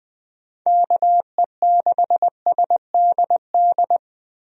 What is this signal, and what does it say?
Morse code, heard as a keyed tone, about 20 wpm, 700 Hz. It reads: KE6SDD
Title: USA Call Signs - 6 Characters - 20wpm